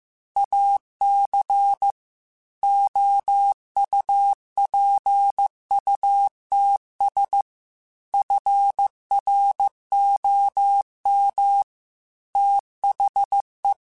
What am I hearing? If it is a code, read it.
Morse code, heard as a keyed tone, about 15 wpm, 800 Hz. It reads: AC OUPUTS FROM THE
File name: Technical Subjects / Technical Subjects-mp3s / Sw Pwr Sup @13